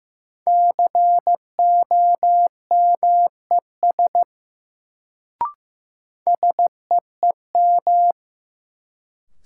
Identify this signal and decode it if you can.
Morse code, heard as a keyed tone, about 15 wpm, 700 Hz. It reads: COMES SEEM
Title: Sets of 3 Words - New Word Drill - 15wpm